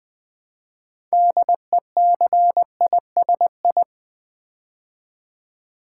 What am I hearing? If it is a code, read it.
Morse code, heard as a keyed tone, about 20 wpm, 700 Hz. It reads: DECISI
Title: Build A Word - 3 Letters At A Time - 8-Letter Words - 20wpm